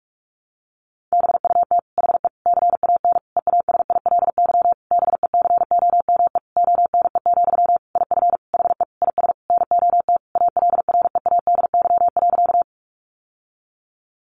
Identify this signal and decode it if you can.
Morse code, heard as a keyed tone, about 44 wpm, 700 Hz. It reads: BUT HE CAN EASILY BECOME ONE, IF HE IS NOT ALREADY.
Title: Sentences from Top 900 Words 45wpm